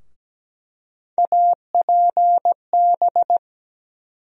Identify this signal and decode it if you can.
Morse code, heard as a keyed tone, about 17 wpm, 700 Hz. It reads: APB
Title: USA Call Sign Postfixes - 3 Letters 17wpm